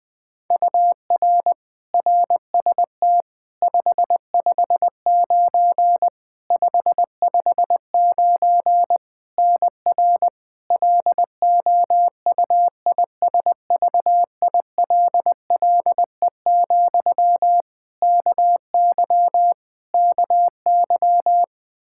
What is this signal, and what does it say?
Morse code, heard as a keyed tone, about 21 wpm, 700 Hz. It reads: UR RST 559 559 NR LOUISVILLE, KY KY
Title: Practice QSO - Exchange 1 - Beginning - 20wpm